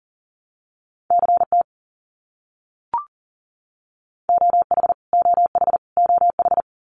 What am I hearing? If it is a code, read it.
Morse code, heard as a keyed tone, about 40 wpm, 700 Hz. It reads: CT OH OH OH
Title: US States Repeated 3x 40wpm